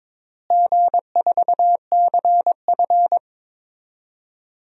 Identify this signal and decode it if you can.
Morse code, heard as a keyed tone, about 21 wpm, 700 Hz. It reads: G4CF
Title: DXCC-100 Call Signs - 4 Characters - Repeated 3x - 22wpm